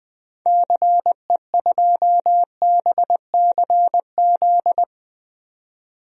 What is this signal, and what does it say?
Morse code, heard as a keyed tone, about 20 wpm, 700 Hz. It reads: CE2BCZ
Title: DXCC-100 Call Signs - 6 Characters - 20wpm